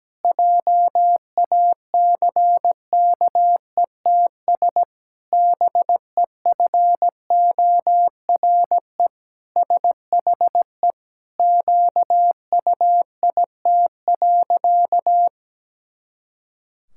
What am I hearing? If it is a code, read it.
Morse code, heard as a keyed tone, about 17 wpm, 700 Hz. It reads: JACKETS BEFORE SHE QUIT.
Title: Pangram Challenge v2 - 17wpm